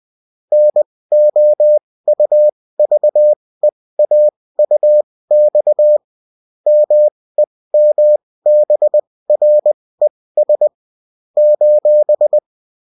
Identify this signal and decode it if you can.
Morse code, heard as a keyed tone, about 20 wpm, spaced as for 16 wpm, 600 Hz. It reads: NOUVEAUX MEMBRES :